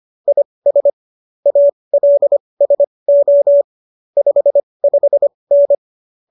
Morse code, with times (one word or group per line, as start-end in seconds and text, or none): 0.3-0.9 IS
1.5-3.6 ALSO
4.2-5.7 55N